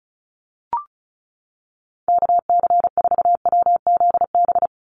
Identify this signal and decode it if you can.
Morse code, heard as a keyed tone, about 34 wpm, 700 Hz. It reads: KC4WZB
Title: NA SOTA Activator Callsigns 35wpm